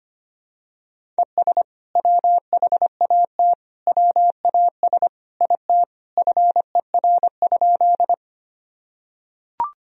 Morse code, heard as a keyed tone, about 25 wpm, 700 Hz. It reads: ES WHAT WAS IT FER?
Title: Sentences from Top 100 Words and Abbreviations 25wpm